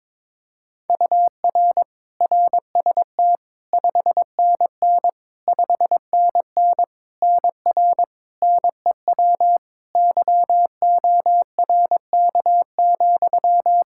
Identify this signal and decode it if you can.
Morse code, heard as a keyed tone, about 22 wpm, 700 Hz. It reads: UR RST 5NN 5NN NR NEW YORK,
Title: Practice QSO - Exchange 1 - Beginning - 22wpm